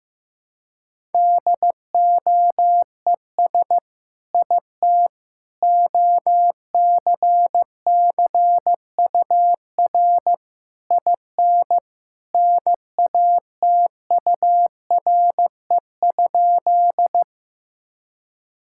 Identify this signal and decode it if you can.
Morse code, heard as a keyed tone, about 15 wpm, 700 Hz. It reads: DOES IT OCCUR IN NATURE?
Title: Sentences from Top 1000 Words - Review Part 6 - 15wpm